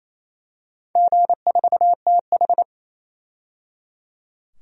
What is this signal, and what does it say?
Morse code, heard as a keyed tone, about 27 wpm, 700 Hz. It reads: G4TH
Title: DXCC-100 Call Signs - 4 Characters - 28wpm